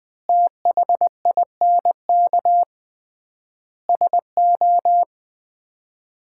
Morse code, heard as a keyed tone, about 20 wpm, 700 Hz. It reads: THINK SO
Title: 3x Word-Spacing - Sets of 5 Words - 20wpm